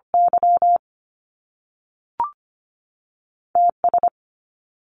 Morse code, heard as a keyed tone, about 25 wpm, 700 Hz. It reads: Y TS
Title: Top Two Letter Combinations in Words 25wpm